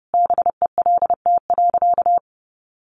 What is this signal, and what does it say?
Morse code, heard as a keyed tone, about 28 wpm, 700 Hz. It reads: BELT.
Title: Sentences from Top 1400 Words 30wpm